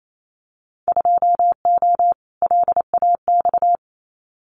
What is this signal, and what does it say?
Morse code, heard as a keyed tone, about 28 wpm, 700 Hz. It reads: 2O LAX
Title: Winter Field Day Exchanges 28wpm